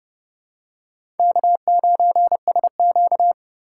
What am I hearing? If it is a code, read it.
Morse code, heard as a keyed tone, about 28 wpm, 690 Hz. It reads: K9SQ
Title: USA Call Signs - 4 Characters - 2x Character Spacing - 30wpm